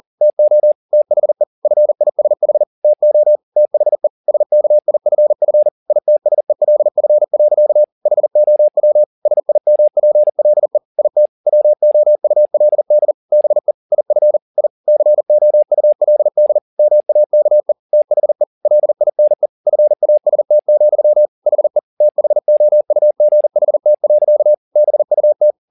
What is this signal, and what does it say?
Morse code, heard as a keyed tone, about 40 wpm, 600 Hz. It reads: TO THE FISH TO THE SKIFF ITSELF. HOW SIMPLE IT WOULD BE IF I COULD MAKE THE LINE FAST, HE THOUGHT. BUT
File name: the old man and the sea 2 40wpm